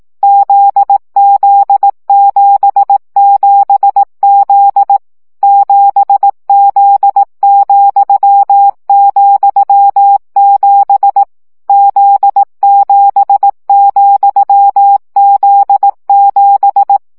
Morse code, heard as a keyed tone, about 18 wpm, 800 Hz. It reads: ZZ77Z 7Z,,7 Z7,Z7
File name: CW gyakorlas z 7 vesszo 15 baud